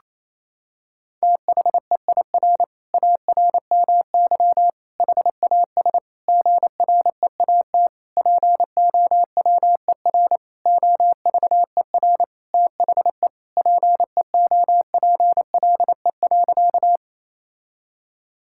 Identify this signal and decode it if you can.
Morse code, heard as a keyed tone, about 28 wpm, 700 Hz. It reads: THEIR ARMY HAS GREAT POWER OVER THE PEOPLE.
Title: Sentences from Top 1000 Words - Review Part 5 - 28wpm